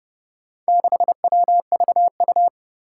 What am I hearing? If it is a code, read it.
Morse code, heard as a keyed tone, about 32 wpm, 700 Hz. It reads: 6WVU
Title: Four Challenging Letter-Number Combinations 30wpm